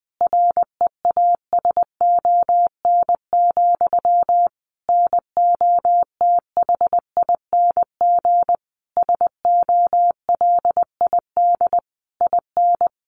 Morse code, heard as a keyed tone, about 20 wpm, 700 Hz. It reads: REASON, NOTHING SOLID IN
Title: Sentences from Top 1500 Words 20wpm